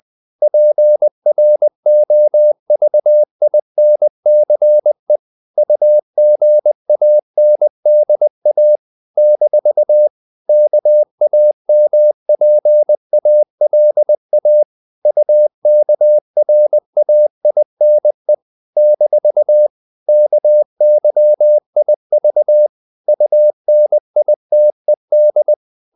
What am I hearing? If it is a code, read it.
Morse code, heard as a keyed tone, about 20 wpm, 600 Hz. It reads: PROVINCE UGANDA - KAMPALA UKRAINE - KYIV UNITED
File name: CW / Practice / NationalCapitals20